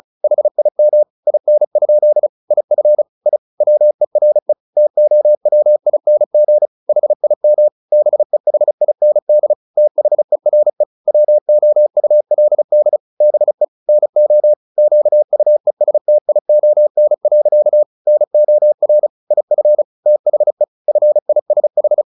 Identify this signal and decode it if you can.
Morse code, heard as a keyed tone, about 34 wpm, 600 Hz. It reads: HIM IN? IF I WERE TOWING HIM BEHIND THERE WOULD BE NO QUESTION. NOR IF THE FISH
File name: the old man and the sea 2 35wpm